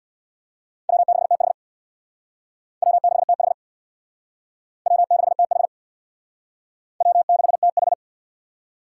Morse code, heard as a keyed tone, about 60 wpm, 700 Hz. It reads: W6TH W6TH W6TH W6TH
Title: Speed Racing - USA 1x2 Call Signs 50wpm